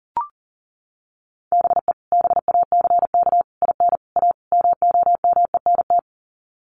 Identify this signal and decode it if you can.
Morse code, heard as a keyed tone, about 40 wpm, 700 Hz. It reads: BE BACK IN A MOMENT
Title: Sets of 5 Words 40wpm